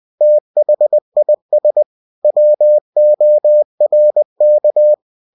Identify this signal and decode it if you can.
Morse code, heard as a keyed tone, about 20 wpm, 600 Hz. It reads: THIS WORK
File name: cw-books / winnie-the-pooh / winnie-20wpm-page-0119